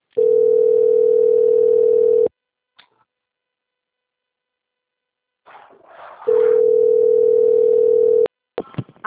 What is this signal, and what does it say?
Ringback tone (North America), 440 + 480 Hz.
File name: out- - - - - .36050